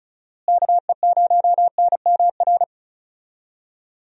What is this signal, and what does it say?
Morse code, heard as a keyed tone, about 36 wpm, 700 Hz. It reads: KE0NMR
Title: USA Call Signs - 6 Characters - Repeated 2x - 35wpm